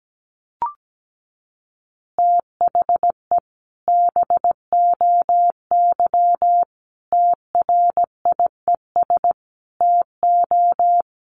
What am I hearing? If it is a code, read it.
Morse code, heard as a keyed tone, about 17 wpm, 700 Hz. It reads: THE BOY TRIES TO
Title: Additional Easy Phrases 17wpm